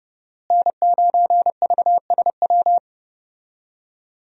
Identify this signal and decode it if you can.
Morse code, heard as a keyed tone, about 32 wpm, 700 Hz. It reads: N9VSW
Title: USA Call Signs - 5 Characters - Repeated 2x - 30wpm